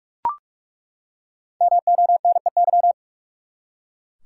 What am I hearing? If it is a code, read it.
Morse code, heard as a keyed tone, about 42 wpm, 700 Hz. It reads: MONEY
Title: Five Letter Words - 45wpm